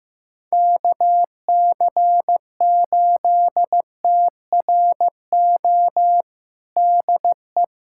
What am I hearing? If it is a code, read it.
Morse code, heard as a keyed tone, about 16 wpm, 700 Hz. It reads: KC8TRO DE
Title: USA Callsign DE Callsign - 6 Characters - 15wpm